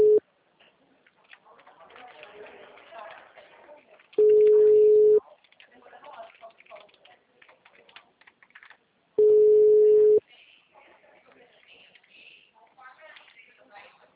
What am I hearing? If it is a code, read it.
Ringback tone (Europe, 425 Hz), with speech.